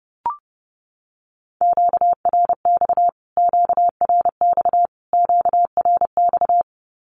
Morse code, heard as a keyed tone, about 30 wpm, 700 Hz. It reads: QRX QRX QRX
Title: Q-Codes Repeated 3x 30wpm